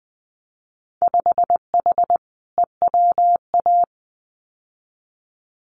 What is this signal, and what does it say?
Morse code, heard as a keyed tone, about 20 wpm, 700 Hz. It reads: 5H EWA
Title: Winter Field Day Exchanges Encore 20wpm